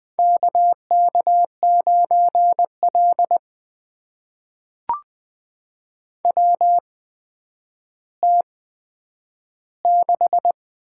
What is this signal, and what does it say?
Morse code, heard as a keyed tone, about 20 wpm, spaced as for 10 wpm, 700 Hz. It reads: KK9L WT6
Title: USA Call Signs - 4 Characters - 8x Character Spacing - 20wpm